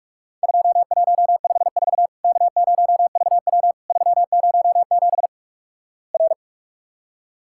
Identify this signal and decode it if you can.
Morse code, heard as a keyed tone, about 46 wpm, 700 Hz. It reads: 2154 K0VW 307 R
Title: Worked All Europe DX Contest - QTC Exchange - 45wpm